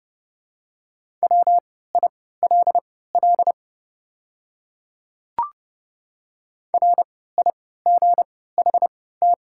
Morse code, heard as a keyed tone, about 30 wpm, spaced as for 17 wpm, 700 Hz. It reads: WILL RIGHT
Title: 3x Character Spacing - Top 200 Words - 30wpm